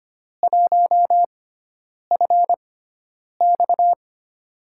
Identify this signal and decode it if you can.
Morse code, heard as a keyed tone, about 25 wpm, spaced as for 9 wpm, 700 Hz. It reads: 1FX
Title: USA Call Signs - 4 Characters - 6x Character Spacing - 25wpm